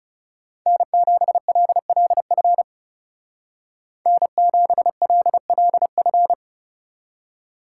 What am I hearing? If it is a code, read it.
Morse code, heard as a keyed tone, about 37 wpm, 700 Hz. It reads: N7LLF N7LLF
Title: Speed Racing - USA 1x3 Call Signs 30wpm